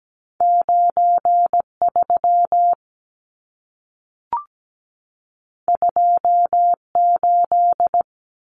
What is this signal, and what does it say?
Morse code, heard as a keyed tone, about 17 wpm, 700 Hz. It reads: 93 28
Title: Numbers - 2-Digit 17wpm